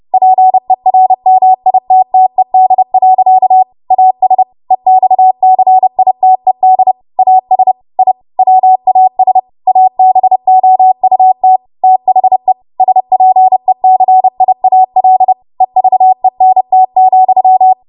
Morse code, heard as a keyed tone, about 30 wpm, 750 Hz. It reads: PERMITTED. AS EXCITED AS I WAS ABOUT THE SPECIAL EVENT,
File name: Morse / 30WPM